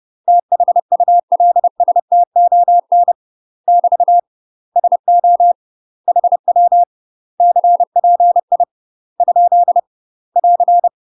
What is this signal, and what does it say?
Morse code, heard as a keyed tone, about 30 wpm, 700 Hz. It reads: THULSTON <BT> SO HW CPI ? <AR>
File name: UK MASTER QSOs TRAINING/30WPM